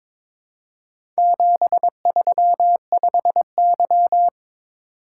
Morse code, heard as a keyed tone, about 22 wpm, 700 Hz. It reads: 735Y
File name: practice / 22wpm / Four Letter-Number Combinations - 22wpm